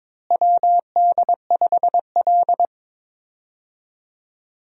Morse code, heard as a keyed tone, about 21 wpm, 700 Hz. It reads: WD5L
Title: USA Call Signs - 4 Characters - Repeated 2x - 22wpm